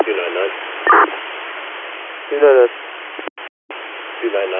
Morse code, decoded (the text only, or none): none